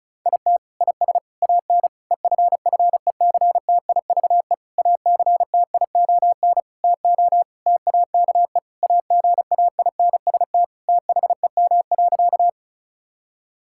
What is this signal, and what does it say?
Morse code, heard as a keyed tone, about 34 wpm, 700 Hz. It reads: IT IS AN EFFECTIVE ACTION TO TAKE AGAINST THEM.
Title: Sentences from Top 1000 Words - Review Part 6 - 35wpm